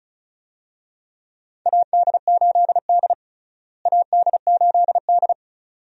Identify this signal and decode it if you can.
Morse code, heard as a keyed tone, about 36 wpm, 700 Hz. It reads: AD8D AD8D
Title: USA Call Signs - 4 Characters - Repeated 2x - 35wpm